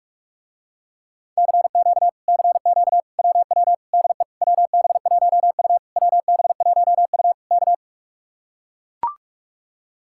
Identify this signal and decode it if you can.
Morse code, heard as a keyed tone, about 45 wpm, 700 Hz. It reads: CQ CQ WW DE WB1U WB1U K